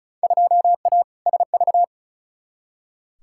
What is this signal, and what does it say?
Morse code, heard as a keyed tone, about 36 wpm, 700 Hz. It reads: 2A SV
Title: Field Day Exchanges - 35wpm